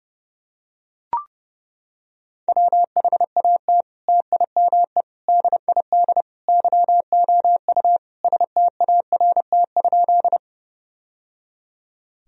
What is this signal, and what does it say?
Morse code, heard as a keyed tone, about 30 wpm, 700 Hz. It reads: WHAT TIME DID YOU START?
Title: Sentences from Top 1000 Words - Review Part 1 - 30wpm